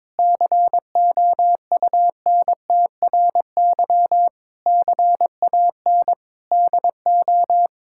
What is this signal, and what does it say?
Morse code, heard as a keyed tone, about 22 wpm, 700 Hz. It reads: COUNTRY CAN DO
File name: Sets of 9 Words - 22wpm